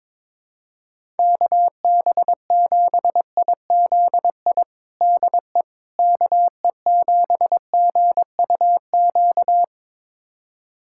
Morse code, heard as a keyed tone, about 22 wpm, 700 Hz. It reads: KB7IZI DE KE7GUQ
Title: USA Callsign DE Callsign - 6 Characters - 22wpm